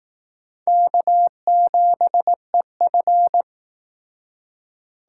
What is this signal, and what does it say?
Morse code, heard as a keyed tone, about 19 wpm, 700 Hz. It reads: K7EF